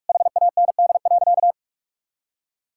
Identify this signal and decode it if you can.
Morse code, heard as a keyed tone, about 46 wpm, 700 Hz. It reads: HAND.
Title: Sentences from Top 1000 Words - Review Part 5 - 45wpm